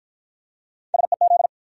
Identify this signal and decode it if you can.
Morse code, heard as a keyed tone, about 49 wpm, 700 Hz. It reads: SEZ